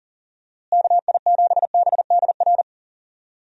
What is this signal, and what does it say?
Morse code, heard as a keyed tone, about 37 wpm, 700 Hz. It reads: KI7BDR
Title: USA Call Signs - 6 Characters - Repeated 3x - 40wpm